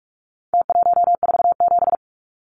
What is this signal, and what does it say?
Morse code, heard as a keyed tone, about 44 wpm, 700 Hz. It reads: T147
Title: Numbers - Contest Style - 4-Digit 45wpm